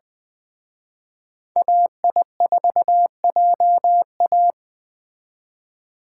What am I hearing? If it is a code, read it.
Morse code, heard as a keyed tone, about 21 wpm, 700 Hz. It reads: AI4JA
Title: Speed Racing - USA 2x2 Call Signs 17wpm